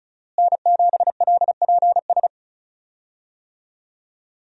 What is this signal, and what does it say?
Morse code, heard as a keyed tone, about 36 wpm, 700 Hz. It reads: N7LPS
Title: USA Call Signs - 5 Characters - Repeated 3x - 35wpm